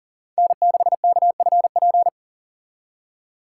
Morse code, heard as a keyed tone, about 50 wpm, 700 Hz. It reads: N6KFP
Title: Speed Racing - USA Call Signs 40wpm